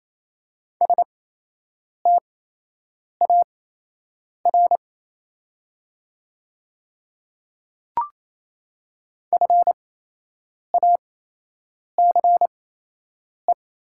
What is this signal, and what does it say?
Morse code, heard as a keyed tone, about 28 wpm, spaced as for 8 wpm, 700 Hz. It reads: STAR FACE
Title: 8x Character Spacing - Top 500 Words - 28wpm